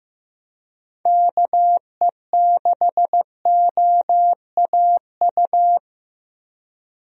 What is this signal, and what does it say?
Morse code, heard as a keyed tone, about 16 wpm, 700 Hz. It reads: KE6OAU
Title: USA Call Signs - 6 Characters - Repeated 3x - 15wpm